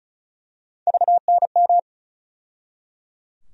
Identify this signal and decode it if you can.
Morse code, heard as a keyed tone, about 34 wpm, 700 Hz. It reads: VNM
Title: USA Call Sign Postfixes - 3 Letters 35wpm